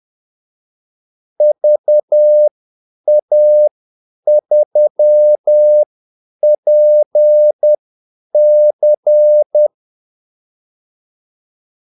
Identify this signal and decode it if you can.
Morse code, heard as a keyed tone, about 10 wpm, spaced as for 8 wpm, 600 Hz. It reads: VA3PC
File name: Callsigns-10-8wpm